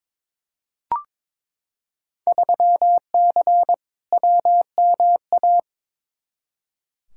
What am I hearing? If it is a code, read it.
Morse code, heard as a keyed tone, about 22 wpm, 700 Hz. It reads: 3C WMA
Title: Field Day Exchanges Encore - 22wpm